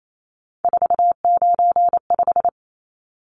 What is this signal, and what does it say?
Morse code, heard as a keyed tone, about 28 wpm, 700 Hz. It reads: 495
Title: Numbers - 3-Digit 28wpm